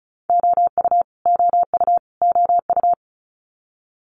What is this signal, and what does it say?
Morse code, heard as a keyed tone, about 36 wpm, 700 Hz. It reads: OU OU OU
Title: Multi-letter Phonemes Repeated 3x 35wpm